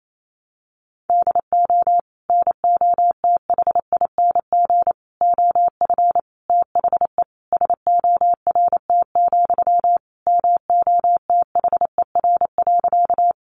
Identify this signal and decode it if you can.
Morse code, heard as a keyed tone, about 28 wpm, 700 Hz. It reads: DO NOTHING OF THE SORT, MOTHER.
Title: Sentences from Top 1300 Words 28wpm